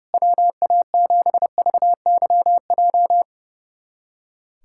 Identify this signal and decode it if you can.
Morse code, heard as a keyed tone, about 29 wpm, 700 Hz. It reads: WA7VYJ
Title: USA Callsign DE Callsign 30wpm